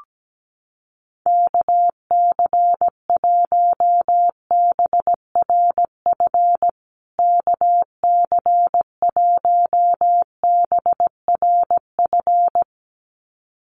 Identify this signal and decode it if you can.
Morse code, heard as a keyed tone, about 17 wpm, 700 Hz. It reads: KC1BRF KC1BRF
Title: USA Call Signs Repeated 2x 17wpm